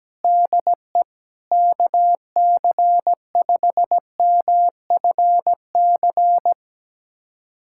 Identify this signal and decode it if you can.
Morse code, heard as a keyed tone, about 17 wpm, 700 Hz. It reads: DE KC5MFC
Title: USA Callsign DE Callsign - 6 Characters - 17wpm